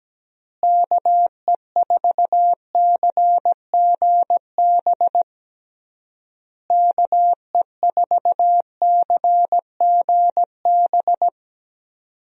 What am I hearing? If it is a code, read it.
Morse code, heard as a keyed tone, about 17 wpm, 700 Hz. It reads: KE4CGB KE4CGB
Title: USA Call Signs - 6 Characters - Repeated 3x - 17wpm